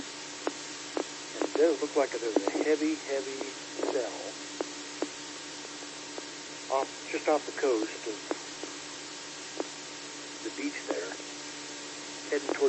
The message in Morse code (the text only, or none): none